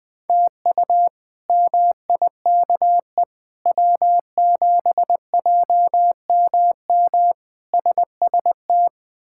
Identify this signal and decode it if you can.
Morse code, heard as a keyed tone, about 20 wpm, 700 Hz. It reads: TU MIKE W7JMM SST
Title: K1USN SST QSO Practice - 20wpm